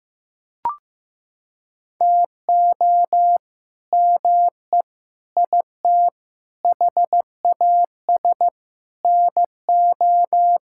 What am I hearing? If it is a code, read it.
Morse code, heard as a keyed tone, about 15 wpm, 700 Hz. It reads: TO ME IT HAS NO
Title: Sentences from Top 1000 Words - Review Part 6 - 15wpm